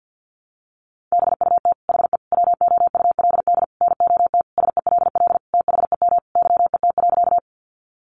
Morse code, heard as a keyed tone, about 50 wpm, 700 Hz. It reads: BUT HE WOULD NOT SELL THEM YET.
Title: Sentences from Top 1100 Words 50wpm